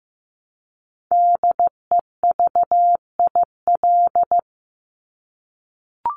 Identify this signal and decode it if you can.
Morse code, heard as a keyed tone, about 15 wpm, 700 Hz. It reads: DEVIL
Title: Top 1751-2000 Words 15wpm